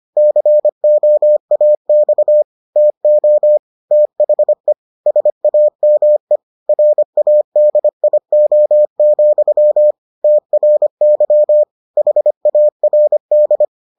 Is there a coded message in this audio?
Yes: Morse code from 0.2 to 13.6 s, heard as a keyed tone, about 25 wpm, 600 Hz. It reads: COAX TO THE SAME RADIO, TRY HARD